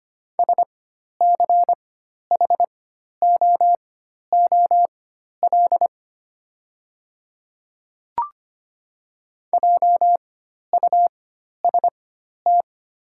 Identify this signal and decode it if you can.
Morse code, heard as a keyed tone, about 25 wpm, spaced as for 12 wpm, 700 Hz. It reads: SCHOOL JUST
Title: 4x Character Spacing - Top 300 Words - 25wpm